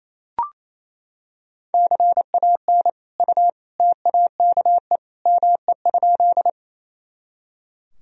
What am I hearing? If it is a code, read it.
Morse code, heard as a keyed tone, about 28 wpm, 700 Hz. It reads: CAN U TAKE ME?
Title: Sentences from Top 200 Words and Abbreviations 28wpm